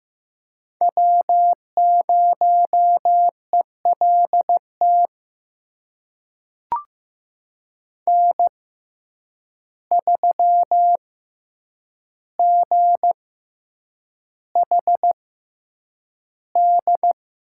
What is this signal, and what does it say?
Morse code, heard as a keyed tone, about 15 wpm, spaced as for 8 wpm, 700 Hz. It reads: W0ELT N3GHD
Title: USA Call Signs - 5 Characters - 6x Character Spacing - 15wpm